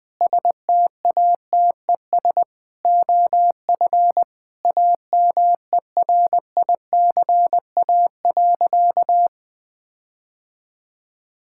Morse code, heard as a keyed tone, about 20 wpm, 700 Hz. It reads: STATES OF AMERICA.
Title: Sentences with 5-Word Phrases - 20wpm